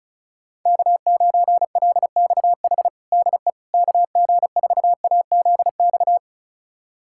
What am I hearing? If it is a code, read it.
Morse code, heard as a keyed tone, about 36 wpm, 700 Hz. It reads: K9LXH DE KG4AZX
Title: USA Callsign DE Callsign 35wpm